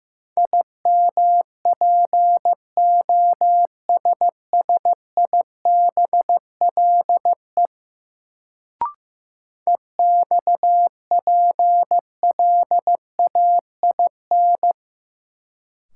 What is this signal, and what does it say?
Morse code, heard as a keyed tone, about 15 wpm, 700 Hz. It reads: IMPOSSIBLE EXPLAIN
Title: Sentences from Top 1100 Words - New Word Drill - 15wpm